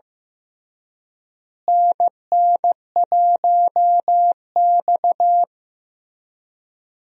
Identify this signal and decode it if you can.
Morse code, heard as a keyed tone, about 15 wpm, 700 Hz. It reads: NN1X
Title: USA Call Signs - 4 Characters - Repeated 2x - 15wpm